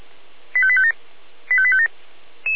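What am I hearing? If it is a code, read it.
Five-tone selective call (ZVEI), heard as quick runs of five tones, tones of about 70 ms. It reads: 85857 85857
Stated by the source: ZVEI five-tone selective call 85810 received on SDR, recorded by feuerwehr-orlamuende (fire brigade Orlamünde).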